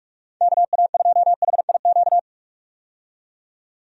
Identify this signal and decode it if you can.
Morse code, heard as a keyed tone, about 41 wpm, 700 Hz. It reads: KA2HIQ
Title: USA Call Signs - 6 Characters - 45wpm